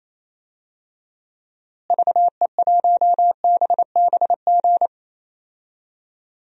Morse code, heard as a keyed tone, about 28 wpm, 700 Hz. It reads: VE1BBG
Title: DXCC-100 Call Signs - 6 Characters - Repeated 3x - 28wpm